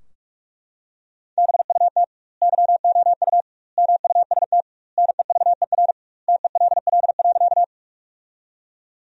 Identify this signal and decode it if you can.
Morse code, heard as a keyed tone, about 41 wpm, 700 Hz. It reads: BUT YOU MUST NEVER TELL.
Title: 2x Word-Spacing - Sentences from Top 300 Words - 45wpm